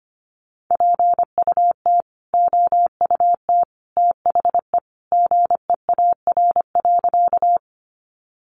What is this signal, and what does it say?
Morse code, heard as a keyed tone, about 25 wpm, 700 Hz. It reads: PUT OUT THE GEAR.
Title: Sentences from Top 1400 Words 25wpm